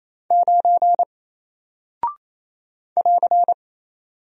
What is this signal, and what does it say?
Morse code, heard as a keyed tone, about 27 wpm, 700 Hz. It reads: 9 <AR>
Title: Single Character (Letter:Number:Punctuation:Prosign) Alt 28wpm